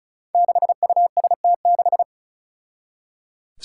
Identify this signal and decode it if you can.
Morse code, heard as a keyed tone, about 36 wpm, 700 Hz. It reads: 6UST6
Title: Five Challenging Letter-Number Combinations 35wpm